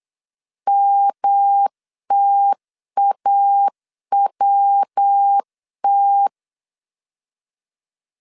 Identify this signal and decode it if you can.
Morse code, heard as a keyed tone, about 8 wpm, 800 Hz. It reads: MTAWT